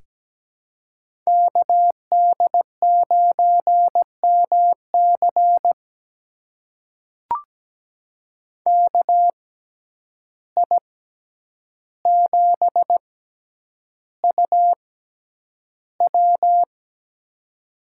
Morse code, heard as a keyed tone, about 17 wpm, spaced as for 9 wpm, 700 Hz. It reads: KD9MC KI7UW
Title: USA Call Signs - 5 Characters - 6x Character Spacing - 17wpm